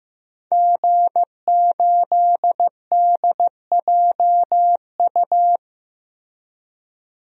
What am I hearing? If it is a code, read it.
Morse code, heard as a keyed tone, about 16 wpm, 700 Hz. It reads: G8DJU
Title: DXCC-100 Call Signs - 5 Characters - Repeated 3x - 15wpm